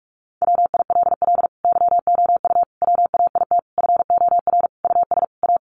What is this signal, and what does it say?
Morse code, heard as a keyed tone, about 44 wpm, 700 Hz. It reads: WILL YOU WAIT FOR US A
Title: Sentences from Top 900 Words 45wpm